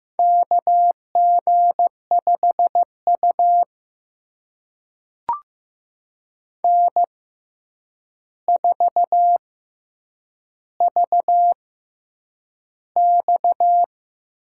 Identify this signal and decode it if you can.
Morse code, heard as a keyed tone, about 15 wpm, spaced as for 8 wpm, 700 Hz. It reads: KG5U N4VX
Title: USA Call Signs - 4 Characters - 6x Character Spacing - 15wpm